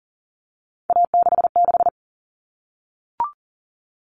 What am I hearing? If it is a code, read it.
Morse code, heard as a keyed tone, about 40 wpm, 700 Hz. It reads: A66
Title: All Cut Numbers - 3-Digit 40wpm